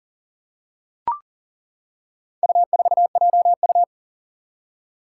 Morse code, heard as a keyed tone, about 40 wpm, 700 Hz. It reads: U4JU
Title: Four Challenging Letter-Number Combinations 40wpm